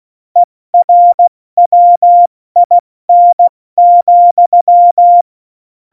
Morse code, heard as a keyed tone, about 16 wpm, spaced as for 14 wpm, 700 Hz. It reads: ERWIN,
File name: Morsekurs / Witze / Witz 03.Juli 201, 70 F80